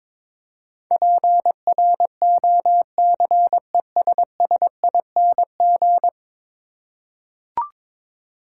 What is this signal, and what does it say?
Morse code, heard as a keyed tone, about 22 wpm, 700 Hz. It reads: PROCESSING
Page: morse-code-ninja-practice-22wpm